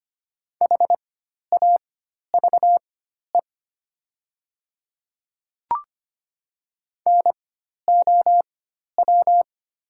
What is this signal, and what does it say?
Morse code, heard as a keyed tone, about 25 wpm, spaced as for 12 wpm, 700 Hz. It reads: HAVE NOW
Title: 4x Character Spacing - Top 100 Words - 25wpm